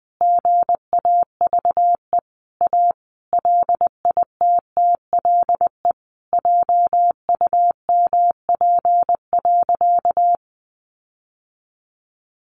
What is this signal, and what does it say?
Morse code, heard as a keyed tone, about 20 wpm, 700 Hz. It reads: GAVE A LITTLE JUMP.
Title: Sentences from Top 1300 Words 20wpm